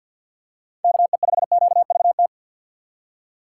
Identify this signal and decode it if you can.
Morse code, heard as a keyed tone, about 48 wpm, 700 Hz. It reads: KE5QVT
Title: USA Call Signs - 6 Characters - 6x Character Spacing - 50wpm